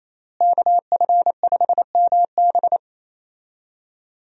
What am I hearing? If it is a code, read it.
Morse code, heard as a keyed tone, about 28 wpm, 700 Hz. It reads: KF5MB
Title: USA Call Signs - 5 Characters - Repeated 2x - 28wpm